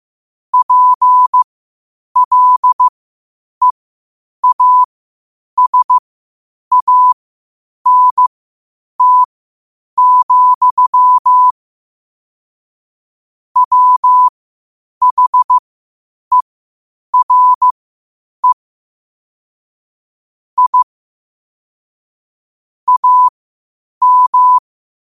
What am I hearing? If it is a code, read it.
Morse code, heard as a keyed tone, about 15 wpm, spaced as for 8 wpm, 1000 Hz. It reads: PLEASANT, WHERE I AM
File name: qotdfiles / 8wpm